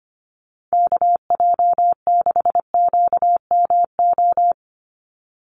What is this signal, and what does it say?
Morse code, heard as a keyed tone, about 25 wpm, 700 Hz. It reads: KJ6QMO
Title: USA Call Signs 25wpm